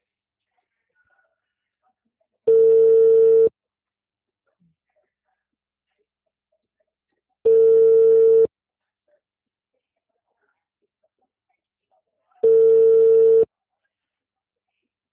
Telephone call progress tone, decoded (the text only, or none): ringback tone